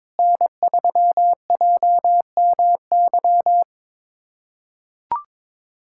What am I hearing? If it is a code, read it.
Morse code, heard as a keyed tone, about 22 wpm, 700 Hz. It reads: N3JMY
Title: USA Callsign DE Callsign 22wpm